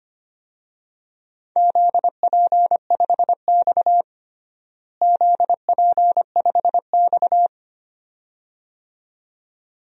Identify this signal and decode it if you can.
Morse code, heard as a keyed tone, about 25 wpm, 700 Hz. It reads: ZP5X ZP5X
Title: DXCC-100 Call Signs - 4 Characters - Repeated 2x - 25wpm